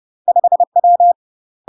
Morse code, heard as a keyed tone, about 30 wpm, 700 Hz. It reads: 5W
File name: UK MASTER QSOs TRAINING / 30WPM / UK MASTER QSOs 0048